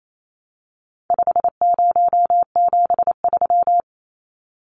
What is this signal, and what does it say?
Morse code, heard as a keyed tone, about 28 wpm, 700 Hz. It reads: 5073
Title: Numbers - 4-Digit 28wpm